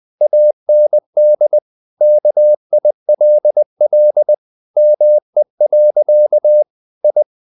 Morse code, heard as a keyed tone, about 20 wpm, 600 Hz. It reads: AND KILL ME. I
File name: the old man and the sea part 2 20wpm